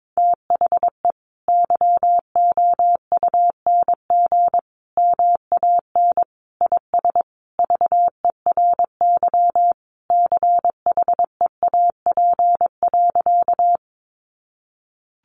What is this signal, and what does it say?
Morse code, heard as a keyed tone, about 22 wpm, 700 Hz. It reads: THE YOUNG MAN IS VERY CHEAP.
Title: Sentences from Top 1200 Words 22wpm